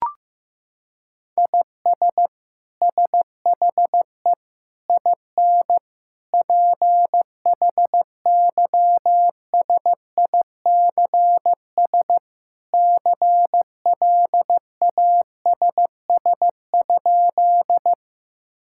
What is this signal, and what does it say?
Morse code, heard as a keyed tone, about 16 wpm, 700 Hz. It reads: IS SHE IN PHYSICS CLASS?
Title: Sentences from Top 1000 Words - Review Part 5 - 15wpm